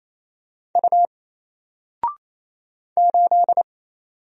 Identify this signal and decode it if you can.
Morse code, heard as a keyed tone, about 27 wpm, 700 Hz. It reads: U 8
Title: Single Character (Letter:Number:Punctuation:Prosign) 28wpm